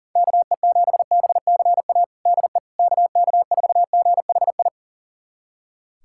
Morse code, heard as a keyed tone, about 40 wpm, 700 Hz. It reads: KE7BCA DE KK4GHI
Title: USA Callsign DE Callsign 40wpm